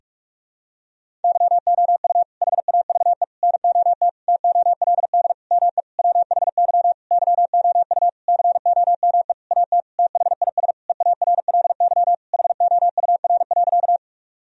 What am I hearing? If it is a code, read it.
Morse code, heard as a keyed tone, about 46 wpm, 700 Hz. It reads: YOU HAVE NOT TOLD ME WHY YOU COME AT THIS EARLY HOUR.
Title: Sentences from Top 1000 Words - Review Part 2 - 45wpm